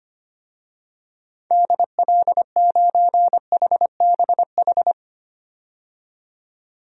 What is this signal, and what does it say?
Morse code, heard as a keyed tone, about 25 wpm, 700 Hz. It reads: DL9HBH
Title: DXCC-100 Call Signs - 6 Characters - 25wpm